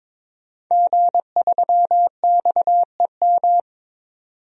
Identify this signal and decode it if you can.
Morse code, heard as a keyed tone, about 22 wpm, 700 Hz. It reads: G3XEM
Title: DXCC-100 Call Signs - 5 Characters - 22wpm